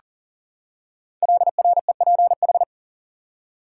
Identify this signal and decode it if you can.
Morse code, heard as a keyed tone, about 39 wpm, 700 Hz. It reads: LREPH
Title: Code Groups - 5 Letters - 40wpm